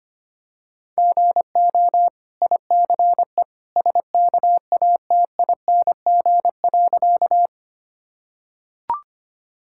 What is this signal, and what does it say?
Morse code, heard as a keyed tone, about 25 wpm, 690 Hz. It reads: GO ICE SKATING.
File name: Easy Sentences with Hints - 25wpm